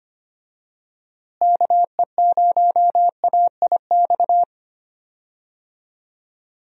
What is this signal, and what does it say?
Morse code, heard as a keyed tone, about 25 wpm, 700 Hz. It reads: KE0AIX
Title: USA Call Signs - 6 Characters - Repeated 2x - 25wpm